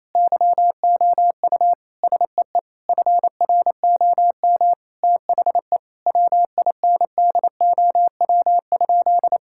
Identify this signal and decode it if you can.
Morse code, heard as a keyed tone, about 28 wpm, 700 Hz. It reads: YOU SEE FROM THE WINDOW?
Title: Sentences from Top 1600 Words 28wpm